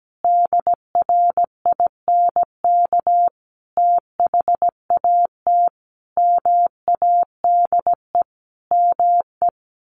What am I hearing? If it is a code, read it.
Morse code, heard as a keyed tone, about 17 wpm, 700 Hz. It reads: DRINK THAT MADE ME
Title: Sentences from Top 1500 Words 17wpm